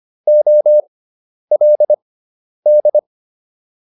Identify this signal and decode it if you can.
Morse code, heard as a keyed tone, about 25 wpm, spaced as for 10 wpm, 600 Hz. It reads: OLD